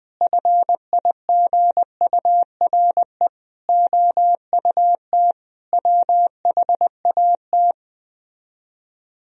Morse code, heard as a keyed tone, about 20 wpm, 690 Hz. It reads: FIGURE OUT WHAT